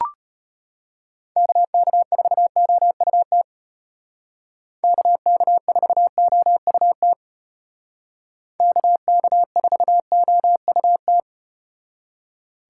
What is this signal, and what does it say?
Morse code, heard as a keyed tone, about 32 wpm, 700 Hz. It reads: KK4OUT KK4OUT KK4OUT